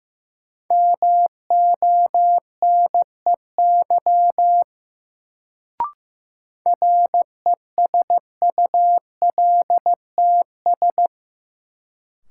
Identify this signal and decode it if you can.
Morse code, heard as a keyed tone, about 15 wpm, 700 Hz. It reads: MONEY RESULTS
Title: Sets of 4 Words - New Word Drill - 15wpm